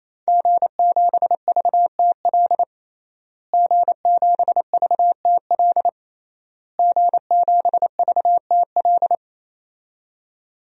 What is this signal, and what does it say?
Morse code, heard as a keyed tone, about 28 wpm, 700 Hz. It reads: G7VTL G7VTL G7VTL
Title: DXCC-100 Call Signs - 5 Characters - Repeated 3x - 28wpm